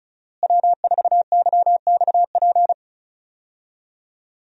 Morse code, heard as a keyed tone, about 36 wpm, 700 Hz. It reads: W4YXP
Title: USA Call Signs - 5 Characters - Repeated 3x - 35wpm